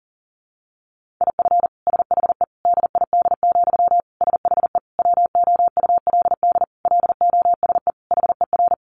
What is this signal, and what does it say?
Morse code, heard as a keyed tone, about 37 wpm, 700 Hz. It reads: IF SHE DID, SHE WOULD LOSE HER
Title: Sentences from Top 1400 Words 40wpm